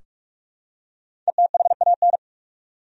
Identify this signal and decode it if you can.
Morse code, heard as a keyed tone, about 45 wpm, 700 Hz. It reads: ETHAN
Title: USA Common Names - Max 5 Letters - 45wpm